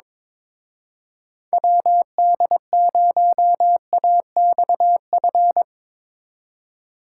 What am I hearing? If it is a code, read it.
Morse code, heard as a keyed tone, about 25 wpm, 700 Hz. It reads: WD0AXF
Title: Speed Racing - USA Call Signs 22wpm